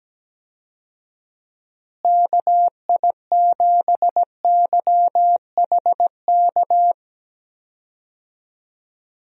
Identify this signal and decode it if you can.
Morse code, heard as a keyed tone, about 17 wpm, 700 Hz. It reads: KI7YHK
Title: USA Call Signs - 6 Characters - Repeated 3x - 17wpm